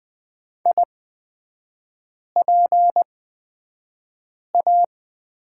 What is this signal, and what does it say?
Morse code, heard as a keyed tone, about 20 wpm, spaced as for 5 wpm, 700 Hz. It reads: IPA
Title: 250 Lesson 40e - BK - Three Characters 20wpm